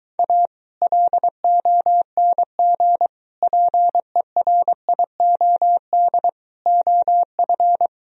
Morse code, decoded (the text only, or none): A LONG PERIOD OF